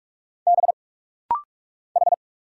Morse code, heard as a keyed tone, about 44 wpm, 700 Hz. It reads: B H
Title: Single Letter-Number Alt - Rapid-Fire 45wpm